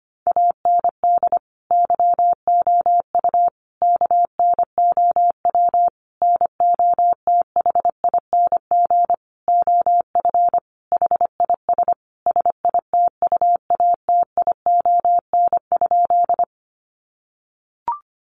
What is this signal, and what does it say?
Morse code, heard as a keyed tone, about 25 wpm, 700 Hz. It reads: AND YOU KNOW NOTHING OF HIS SITUATION?
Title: Sentences from Top 1200 Words 25wpm